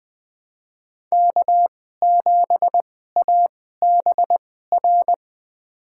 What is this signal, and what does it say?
Morse code, heard as a keyed tone, about 20 wpm, spaced as for 15 wpm, 700 Hz. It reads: K7ABR
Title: USA Call Signs - 5 Characters - 2x Character Spacing - 20wpm